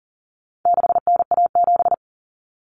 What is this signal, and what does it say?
Morse code, heard as a keyed tone, about 40 wpm, 700 Hz. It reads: 6NA7
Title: Common Cut Numbers - 4-Digit 40wpm